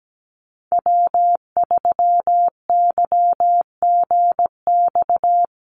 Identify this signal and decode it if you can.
Morse code, heard as a keyed tone, about 17 wpm, 700 Hz. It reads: W3YGX
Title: USA Call Signs Repeated 3x 17wpm